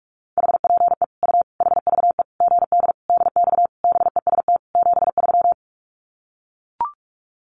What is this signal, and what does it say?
Morse code, heard as a keyed tone, about 44 wpm, 700 Hz. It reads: HPE U HVE GD DX BEST 73
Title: Common QSO Phrases 45wpm